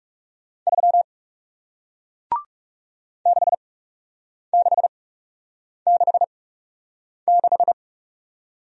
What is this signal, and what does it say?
Morse code, heard as a keyed tone, about 41 wpm, spaced as for 8 wpm, 700 Hz. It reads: 3 6666